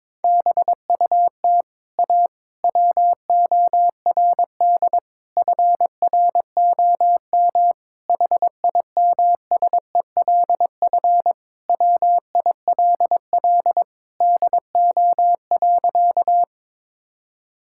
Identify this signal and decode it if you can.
Morse code, heard as a keyed tone, about 21 wpm, 700 Hz. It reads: BUT A WORD FROM HIMSELF WILL DO.
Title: Sentences from Top 1600 Words 22wpm